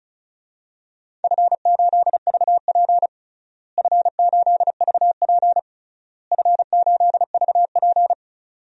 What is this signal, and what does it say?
Morse code, heard as a keyed tone, about 36 wpm, 700 Hz. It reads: F8VP F8VP F8VP
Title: DXCC-100 Call Signs - 4 Characters - Repeated 3x - 35wpm